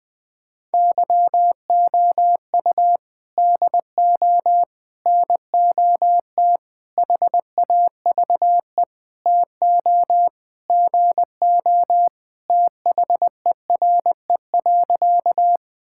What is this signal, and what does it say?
Morse code, heard as a keyed tone, about 20 wpm, 700 Hz. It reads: YOU DO NOT HAVE TO GO THERE.
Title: Sentences with 5-Word Phrases - 20wpm